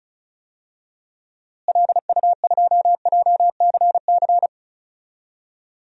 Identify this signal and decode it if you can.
Morse code, heard as a keyed tone, about 36 wpm, 700 Hz. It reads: LU2JCC
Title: DXCC-100 Call Signs - 6 Characters - Repeated 2x - 35wpm